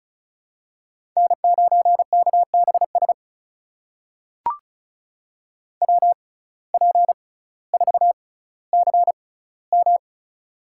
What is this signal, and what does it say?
Morse code, heard as a keyed tone, about 36 wpm, spaced as for 18 wpm, 700 Hz. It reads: N9KBS WP4CM
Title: USA Call Signs - 5 Characters - 6x Character Spacing - 35wpm